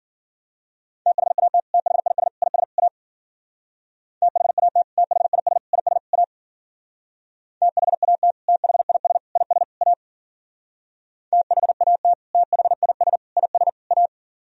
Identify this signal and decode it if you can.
Morse code, heard as a keyed tone, about 53 wpm, 700 Hz. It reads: THAT THIS IS A THAT THIS IS A THAT THIS IS A THAT THIS IS A